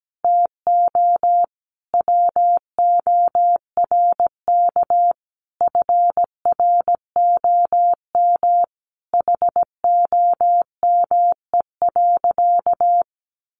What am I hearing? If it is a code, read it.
Morse code, heard as a keyed tone, about 17 wpm, 700 Hz. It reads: TO WORK FROM HOME.
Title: Sentences from Top 1500 Words 17wpm